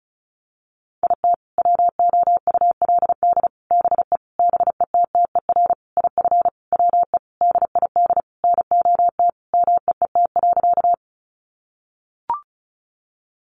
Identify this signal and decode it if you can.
Morse code, heard as a keyed tone, about 34 wpm, 700 Hz. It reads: IT WOULD BE BETTER IF WE DID NOT MEET.
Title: Sentences from Top 1100 Words 35wpm